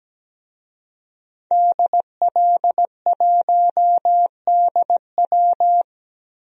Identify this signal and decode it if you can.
Morse code, heard as a keyed tone, about 17 wpm, 700 Hz. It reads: DL1DW
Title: DXCC-100 Call Signs - 5 Characters - 17wpm